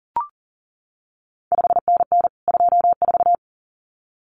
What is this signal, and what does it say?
Morse code, heard as a keyed tone, about 41 wpm, 700 Hz. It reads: 5NN 24
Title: CQ WW DX Contest Exchange Encore - 40wpm